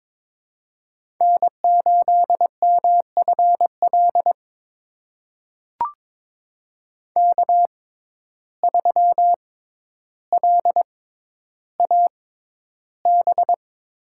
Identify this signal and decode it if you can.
Morse code, heard as a keyed tone, about 22 wpm, spaced as for 11 wpm, 700 Hz. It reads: N8MFL K3LAB
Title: USA Call Signs - 5 Characters - 6x Character Spacing - 22wpm